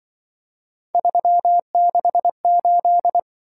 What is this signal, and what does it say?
Morse code, heard as a keyed tone, about 23 wpm, 700 Hz. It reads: 368
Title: Speed-Racing - Three Letter-Number Combinations - 20wpm